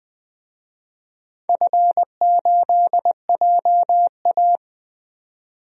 Morse code, heard as a keyed tone, about 20 wpm, 700 Hz. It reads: F8JA
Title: DXCC-100 Call Signs - 4 Characters - 20wpm